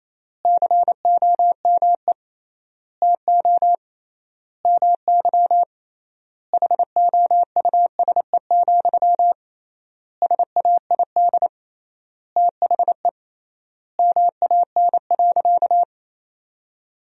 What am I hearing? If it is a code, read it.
Morse code, heard as a keyed tone, about 28 wpm, 700 Hz. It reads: COME TO MY HOUSE, SAID THE MAN.
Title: 3x Word-Spacing - Sentences from Top 400 Words - 28wpm